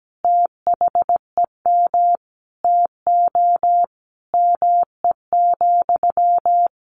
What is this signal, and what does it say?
Morse code, heard as a keyed tone, about 17 wpm, 700 Hz. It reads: THEM TO ME,
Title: Sentences from Top 500 Words 17wpm